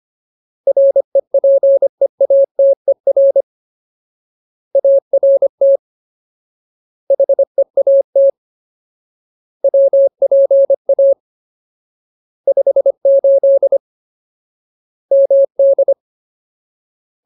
Morse code, heard as a keyed tone, about 25 wpm, 550 Hz. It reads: REPEATER ART HEAT WPA 58 MD
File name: Ultimate Word List / 25WPM Ultimate